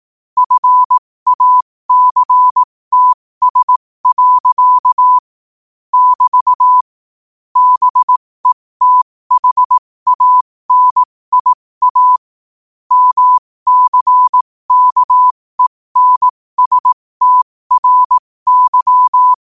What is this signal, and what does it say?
Morse code, heard as a keyed tone, about 18 wpm, spaced as for 15 wpm, 1000 Hz. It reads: FACTS. <BT> BETHANIA MCKENSTRY
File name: qotdfiles / 15wpm